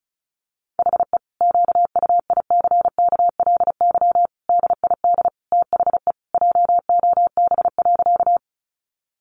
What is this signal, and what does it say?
Morse code, heard as a keyed tone, about 34 wpm, 700 Hz. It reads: HE QUICKLY DID THE JOB.
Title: Sentences from Top 700 Words 35wpm